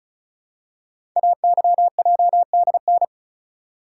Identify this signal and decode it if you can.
Morse code, heard as a keyed tone, about 36 wpm, 690 Hz. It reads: AYJDN
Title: Code Groups - 5 Letters - 35wpm